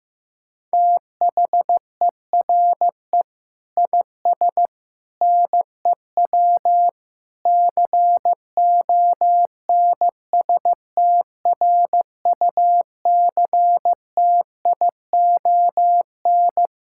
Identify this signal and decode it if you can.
Morse code, heard as a keyed tone, about 16 wpm, 700 Hz. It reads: THERE IS NEW CONSTRUCTION
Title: Sentences from Top 1600 Words 15wpm